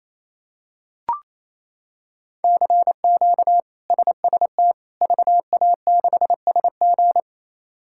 Morse code, heard as a keyed tone, about 28 wpm, 700 Hz. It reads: CQ SST VA6SG
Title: K1USN SST QSO Practice - 28wpm